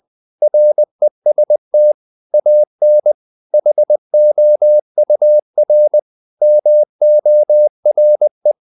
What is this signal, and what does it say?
Morse code, heard as a keyed tone, about 20 wpm, 600 Hz. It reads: REST AN HOUR MORE
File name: the old man and the sea part 2 20wpm